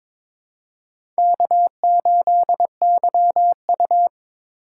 Morse code, heard as a keyed tone, about 22 wpm, 700 Hz. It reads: K8YU
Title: USA Call Signs - 4 Characters - 3x Character Spacing - 22wpm